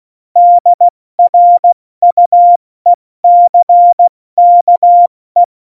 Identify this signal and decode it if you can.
Morse code, heard as a keyed tone, about 16 wpm, 700 Hz. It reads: DRUECKE
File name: Morsekurs / Witze / Witz 09. Januar 2018, 70 F80